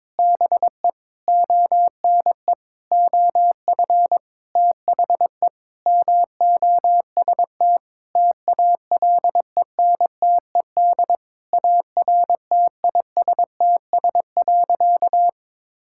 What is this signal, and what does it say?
Morse code, heard as a keyed tone, about 22 wpm, 700 Hz. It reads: BE ONE OF THE MOST TALENTED ARTISTS.
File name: practice / 22wpm / Sentences with 8-Word Phrases - 22wpm